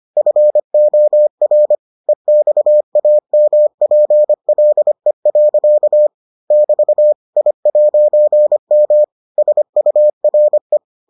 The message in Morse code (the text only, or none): FOR EXAMPLE. <BT> I'M SURE